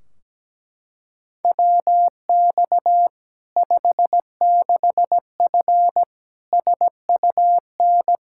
Morse code, heard as a keyed tone, about 17 wpm, 700 Hz. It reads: WX 56F SUN
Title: Practice QSO - Exchange 2 - Info - 17wpm